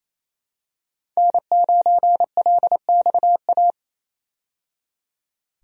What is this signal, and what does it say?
Morse code, heard as a keyed tone, about 28 wpm, 700 Hz. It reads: N9LXA
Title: USA Call Signs - 5 Characters - 28wpm